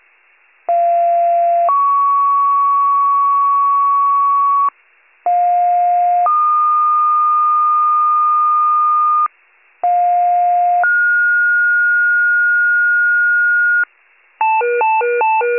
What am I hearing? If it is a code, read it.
Two-tone page, heard as a one-second tone, then a three-second tone, about 690 then 1120 Hz, about 710 then 1220 Hz, about 710 then 1430 Hz. It is heard from 0.7 to 13.8 s.